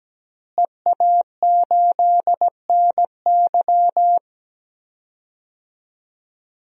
Morse code, heard as a keyed tone, about 17 wpm, 700 Hz. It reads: EA8NY
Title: DXCC-100 Call Signs - 5 Characters - Repeated 2x - 17wpm